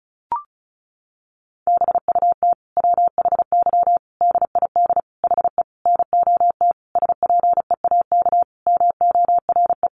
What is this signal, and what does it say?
Morse code, heard as a keyed tone, about 34 wpm, 700 Hz. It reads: BUT WHY DID HE NOT SPEAK MORE
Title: Sentences from Top 1300 Words 35wpm